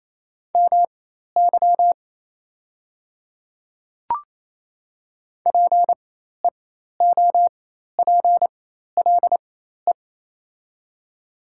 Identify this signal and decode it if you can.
Morse code, heard as a keyed tone, about 28 wpm, spaced as for 13 wpm, 700 Hz. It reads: MY PEOPLE
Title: 4x Character Spacing - Top 100 Words - 28wpm